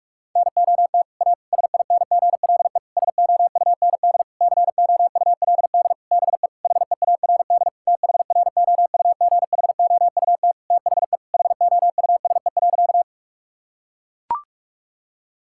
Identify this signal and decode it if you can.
Morse code, heard as a keyed tone, about 41 wpm, 700 Hz. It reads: NOT A SINGLE SOUND COULD BE HEARD THROUGHOUT THE HOUSE.
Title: Sentences from Top 1000 Words - Review Part 5 - 45wpm